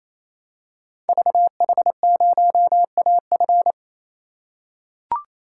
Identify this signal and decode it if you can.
Morse code, heard as a keyed tone, about 28 wpm, 700 Hz. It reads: VH0AF
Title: Code Groups - 5 Letter:Numbers - 28wpm